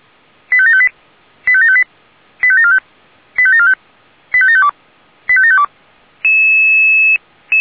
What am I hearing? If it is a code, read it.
Five-tone selective call (ZVEI), heard as quick runs of five tones, tones of about 70 ms. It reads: 85758 85758 85745 85745 85752 85752
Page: /464/SDR/ZVEI/